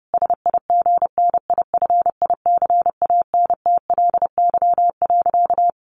Morse code, heard as a keyed tone, about 30 wpm, 700 Hz. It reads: SIGNIFICANTLY.
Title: Sentences from Top 1500 Words 30wpm